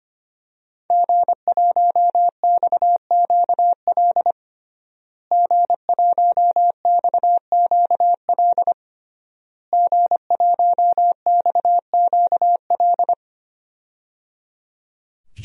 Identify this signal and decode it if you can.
Morse code, heard as a keyed tone, about 25 wpm, 700 Hz. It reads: G1XQL G1XQL G1XQL
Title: DXCC-100 Call Signs - 5 Characters - Repeated 3x - 25wpm